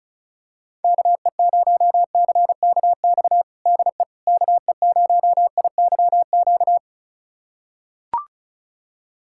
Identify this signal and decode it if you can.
Morse code, heard as a keyed tone, about 34 wpm, 690 Hz. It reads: KE0CKX DE KE0IYQ
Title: USA Callsign DE Callsign - 6 Characters - 35wpm